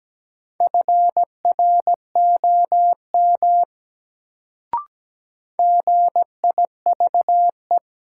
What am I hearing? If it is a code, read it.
Morse code, heard as a keyed tone, about 17 wpm, 700 Hz. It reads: FROM GIVE
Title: Sets of 2 Words - New Word Drill - 17wpm